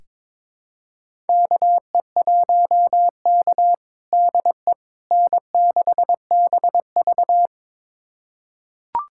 Morse code, heard as a keyed tone, about 22 wpm, 700 Hz. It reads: KE1K DE N6BV
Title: USA Callsign DE Callsign - 4 Characters - 22wpm